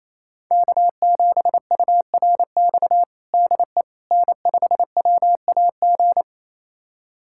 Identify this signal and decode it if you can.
Morse code, heard as a keyed tone, about 27 wpm, 700 Hz. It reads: K7URX DE N5WAG
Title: USA Callsign DE Callsign - 5 Characters - 28wpm